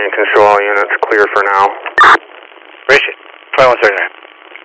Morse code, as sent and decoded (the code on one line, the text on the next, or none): none